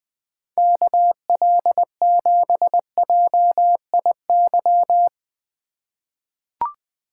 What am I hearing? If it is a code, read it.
Morse code, heard as a keyed tone, about 20 wpm, 700 Hz. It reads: KL7JIY
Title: USA Call Signs - 6 Characters - 4x Character Spacing - 20wpm